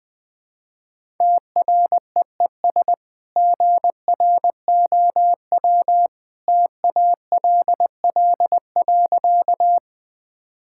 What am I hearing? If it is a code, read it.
Morse code, heard as a keyed tone, about 20 wpm, 700 Hz. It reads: TREES GROW TALL.
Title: Easy Sentences with Hints - 20wpm